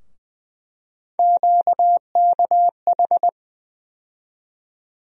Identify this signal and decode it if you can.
Morse code, heard as a keyed tone, about 20 wpm, 700 Hz. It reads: QKH
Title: USA Call Sign Postfixes - 3 Letters 20wpm